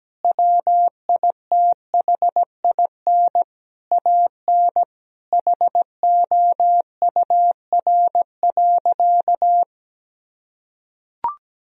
Morse code, heard as a keyed tone, about 17 wpm, 700 Hz. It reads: WITHIN AN HOUR.
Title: Sentences from Top 1000 Words - Review Part 4 - 17wpm